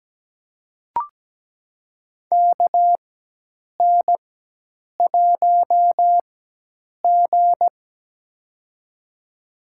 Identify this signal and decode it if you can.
Morse code, heard as a keyed tone, about 17 wpm, spaced as for 8 wpm, 700 Hz. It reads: KN1G
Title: USA Call Signs - 4x Character Spacing - 17wpm